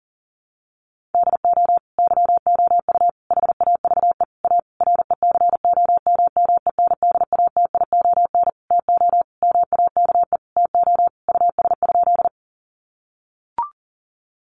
Morse code, heard as a keyed tone, about 40 wpm, 700 Hz. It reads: DO YOU HAVE A RECOMMENDATION TO MAKE TO US?
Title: Sentences from Top 1400 Words 40wpm